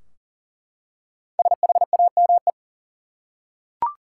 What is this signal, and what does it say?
Morse code, heard as a keyed tone, about 40 wpm, 700 Hz. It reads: SHAME
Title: Five Letter Words - 40wpm